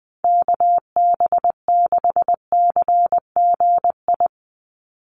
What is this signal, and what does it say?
Morse code, heard as a keyed tone, about 20 wpm, 700 Hz. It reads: KB6CGI
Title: USA Call Signs 20wpm